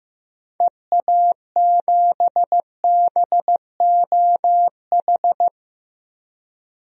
Morse code, heard as a keyed tone, about 16 wpm, 700 Hz. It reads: EA7BOH
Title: DXCC-100 Call Signs - 6 Characters - Repeated 3x - 15wpm